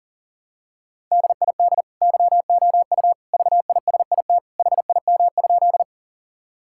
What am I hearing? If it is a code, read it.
Morse code, heard as a keyed tone, about 40 wpm, 700 Hz. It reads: DID YOU VISIT HIM?
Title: Sentences from Top 1000 Words - Review Part 6 - 40wpm